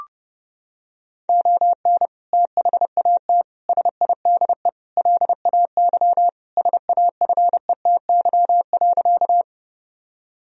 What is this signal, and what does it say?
Morse code, heard as a keyed tone, about 30 wpm, 700 Hz. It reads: ON THAT SIDE LAY SAFETY.
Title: Sentences from Top 1000 Words - Review Part 4 - 30wpm